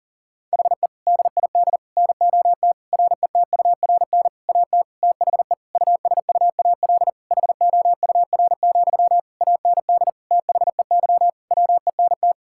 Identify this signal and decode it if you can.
Morse code, heard as a keyed tone, about 40 wpm, 700 Hz. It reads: HE DID NOT RETURN AT THE USUAL HOUR, AND THEY WENT
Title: Sentences from Top 1600 Words 40wpm